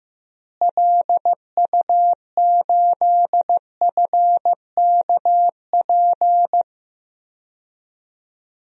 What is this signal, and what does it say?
Morse code, heard as a keyed tone, about 16 wpm, 700 Hz. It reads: LU8FKP
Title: DXCC-100 Call Signs - 6 Characters - Repeated 3x - 15wpm